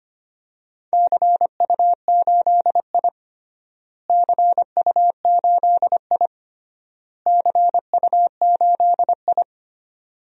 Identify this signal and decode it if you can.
Morse code, heard as a keyed tone, about 25 wpm, 700 Hz. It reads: CU8I CU8I CU8I
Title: DXCC-100 Call Signs - 4 Characters - Repeated 3x - 25wpm